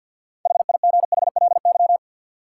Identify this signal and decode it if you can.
Morse code, heard as a keyed tone, about 53 wpm, 700 Hz. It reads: HIGHLY
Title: Sentences from Top 1100 Words - New Word Drill Alt - 50wpm